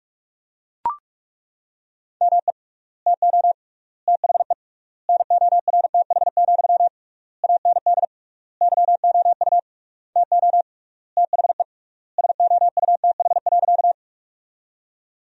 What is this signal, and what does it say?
Morse code, heard as a keyed tone, about 41 wpm, 700 Hz. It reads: ME TO THE NORTH, AND YOU TO THE SOUTH.
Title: 3x Word-Spacing - Sentences from Top 400 Words - 45wpm